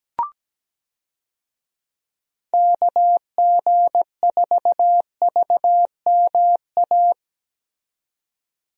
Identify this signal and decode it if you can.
Morse code, heard as a keyed tone, about 17 wpm, 700 Hz. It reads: KG4VMA
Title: USA Call Signs - 6 Characters - 17wpm